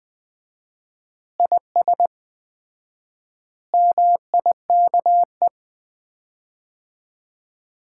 Morse code, heard as a keyed tone, about 20 wpm, 700 Hz. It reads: IS MIKE
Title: Common QSO Phrases - 4x Word-Spacing - 20wpm